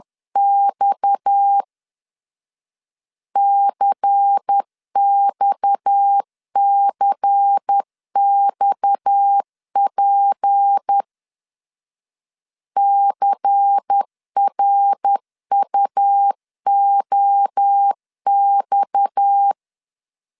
Morse code, heard as a keyed tone, about 10 wpm, 800 Hz. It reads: X CXCXP CRUOX
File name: LEZIONI / 6 PKX0 / 6 PKX 10wpm 2426